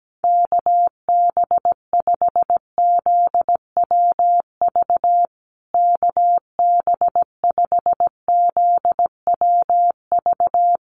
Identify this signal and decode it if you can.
Morse code, heard as a keyed tone, about 17 wpm, 700 Hz. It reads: KB5ZWV KB5ZWV
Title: USA Call Signs Repeated 3x 17wpm